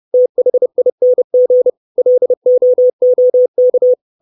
Morse code, heard as a keyed tone, about 30 wpm, 500 Hz. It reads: THING LOOK
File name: 632 Words 30WPM